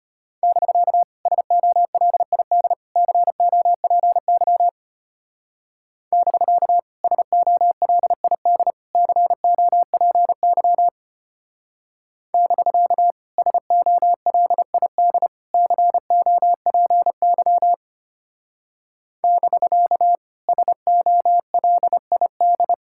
Morse code, heard as a keyed tone, about 32 wpm, 700 Hz. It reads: <BK> SOLID COPY <BK> SOLID COPY <BK> SOLID COPY <BK> SOLID